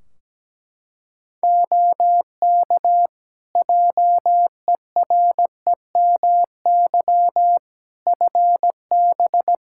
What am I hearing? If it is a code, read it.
Morse code, heard as a keyed tone, about 17 wpm, 700 Hz. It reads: OK JEREMY FB
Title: Practice QSO - Exchange 2 - Info - 17wpm